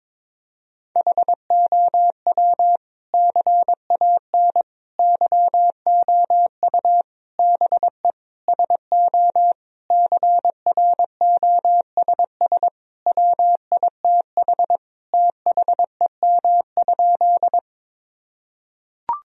Morse code, heard as a keyed tone, about 22 wpm, 700 Hz. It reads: HOW CAN YOU BE SO CROSS WITH THEM?
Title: Sentences from Top 1000 Words - Review Part 1 - 22wpm